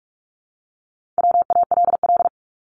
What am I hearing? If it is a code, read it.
Morse code, heard as a keyed tone, about 44 wpm, 700 Hz. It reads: WALL
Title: Top 1001-1250 Words 45wpm